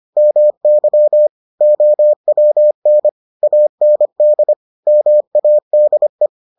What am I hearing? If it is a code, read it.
Morse code, heard as a keyed tone, about 25 wpm, 600 Hz. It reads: MY OWN AND MADE